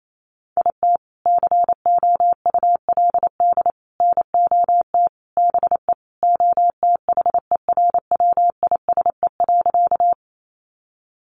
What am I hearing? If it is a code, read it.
Morse code, heard as a keyed tone, about 28 wpm, 700 Hz. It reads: IT COULD NOT BE OTHERWISE.
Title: Sentences from Top 1200 Words 28wpm